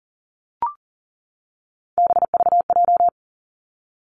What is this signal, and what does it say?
Morse code, heard as a keyed tone, about 41 wpm, 700 Hz. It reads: BVJ
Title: Three Challenging Letter-Number Combinations 40wpm